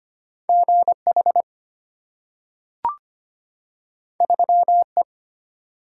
Morse code, heard as a keyed tone, about 25 wpm, 690 Hz. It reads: GH 3E
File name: Two Letter-Number Combinations - 25wpm